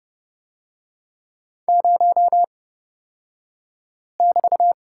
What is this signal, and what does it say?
Morse code, heard as a keyed tone, about 30 wpm, spaced as for 5 wpm, 700 Hz. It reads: 0<BT>
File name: Lesson 36d - BT - Two Characters 30wpm